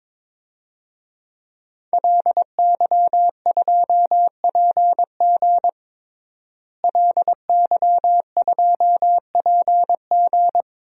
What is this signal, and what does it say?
Morse code, heard as a keyed tone, about 22 wpm, 700 Hz. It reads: LY2PG LY2PG
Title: DXCC-100 Call Signs - 5 Characters - Repeated 2x - 22wpm